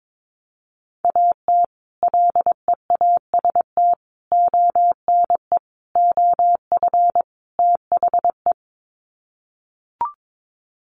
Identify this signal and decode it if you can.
Morse code, heard as a keyed tone, about 22 wpm, 700 Hz. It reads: AT LEAST ONE OF THE
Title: Sets of 5 Words Encore 22wpm